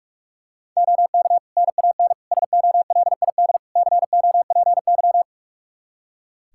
Morse code, heard as a keyed tone, about 46 wpm, 700 Hz. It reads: OK NAN SOLID COPY
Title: Practice QSO - Exchange 3 - More Info - 45wpm